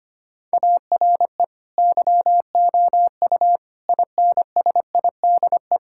Morse code, heard as a keyed tone, about 25 wpm, 700 Hz. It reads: ARE YOU INSIDE
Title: Sentences from Top 1000 Words - Review Part 6 - 25wpm